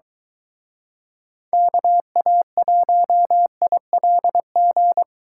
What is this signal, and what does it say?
Morse code, heard as a keyed tone, about 23 wpm, 700 Hz. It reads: KA1ILG